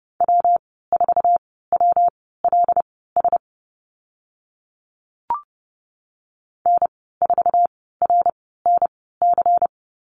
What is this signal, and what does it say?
Morse code, heard as a keyed tone, about 30 wpm, spaced as for 17 wpm, 700 Hz. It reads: W4WLS N4RNC
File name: Lesson 19g - Letter w - Call Signs 30wpm